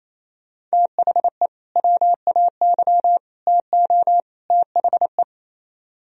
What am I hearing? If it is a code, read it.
Morse code, heard as a keyed tone, about 36 wpm, 700 Hz. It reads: THE WAY TO THE